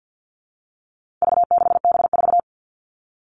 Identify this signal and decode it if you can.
Morse code, heard as a keyed tone, about 50 wpm, 700 Hz. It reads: V6B4
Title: All Cut Numbers - 4-Digit 50wpm